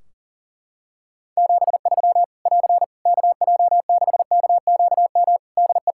Morse code, heard as a keyed tone, about 40 wpm, 700 Hz. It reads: 73 <AR> KJ6KQM DE
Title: Practice QSO - Exchange 4 - Ending - 40wpm